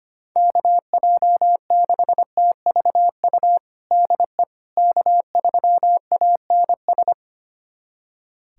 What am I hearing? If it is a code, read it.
Morse code, heard as a keyed tone, about 25 wpm, 700 Hz. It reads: KJ6TVU DE K3ANS
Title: USA Callsign DE Callsign 25wpm